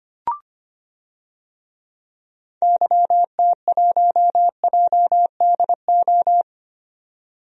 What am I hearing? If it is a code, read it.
Morse code, heard as a keyed tone, about 25 wpm, 700 Hz. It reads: YT1JDO
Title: DXCC-100 Call Signs - 6 Characters - 25wpm